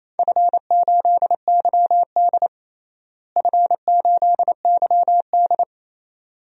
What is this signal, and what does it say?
Morse code, heard as a keyed tone, about 28 wpm, 700 Hz. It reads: F8YD F8YD
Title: DXCC-100 Call Signs - 4 Characters - Repeated 2x - 28wpm